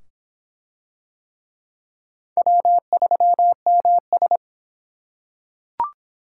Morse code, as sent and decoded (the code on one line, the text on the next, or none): .-- ...-- -- ...
W3MS